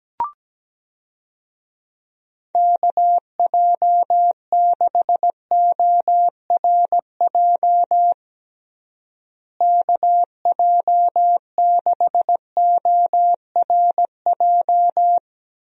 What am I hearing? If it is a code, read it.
Morse code, heard as a keyed tone, about 17 wpm, 700 Hz. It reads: KJ6ORJ KJ6ORJ
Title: USA Call Signs - 6 Characters - Repeated 2x - 17wpm